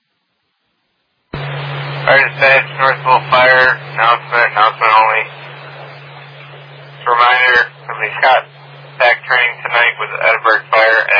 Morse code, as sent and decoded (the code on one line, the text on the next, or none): none